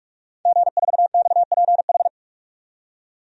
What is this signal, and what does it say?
Morse code, heard as a keyed tone, about 42 wpm, 700 Hz. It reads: G4XPH
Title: DXCC-100 Call Signs - 5 Characters - 45wpm